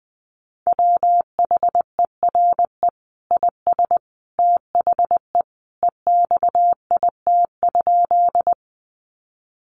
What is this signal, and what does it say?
Morse code, heard as a keyed tone, about 20 wpm, 700 Hz. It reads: WHERE IS THE EXIT?
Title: Sentences from Top 1100 Words 20wpm